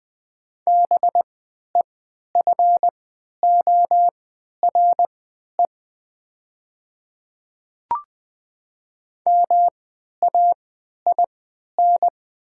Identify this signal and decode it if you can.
Morse code, heard as a keyed tone, about 20 wpm, spaced as for 11 wpm, 700 Hz. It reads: BEFORE MAIN
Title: 3x Character Spacing - Top 300 Words - 20wpm